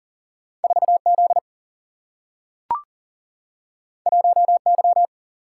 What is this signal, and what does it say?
Morse code, heard as a keyed tone, about 40 wpm, 700 Hz. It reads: 4 Z 1 Y
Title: Single-Letter Single-Number 40wpm